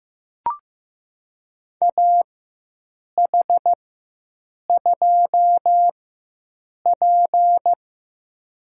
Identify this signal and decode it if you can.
Morse code, heard as a keyed tone, about 15 wpm, spaced as for 7 wpm, 700 Hz. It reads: AH2P
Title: USA Call Signs - 4 Characters - 4x Character Spacing - 15wpm